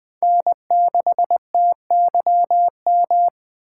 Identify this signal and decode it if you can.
Morse code, heard as a keyed tone, about 20 wpm, 700 Hz. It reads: N6TYM